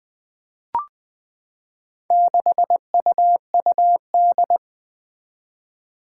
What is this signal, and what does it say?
Morse code, heard as a keyed tone, about 20 wpm, 700 Hz. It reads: 6UUD
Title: Four Challenging Letter-Number Combinations 20wpm